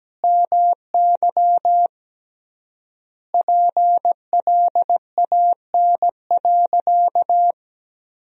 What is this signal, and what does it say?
Morse code, heard as a keyed tone, about 17 wpm, 700 Hz. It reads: MY PLAN.
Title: 3x Word-Spacing - Sentences from Top 500 Words - 17wpm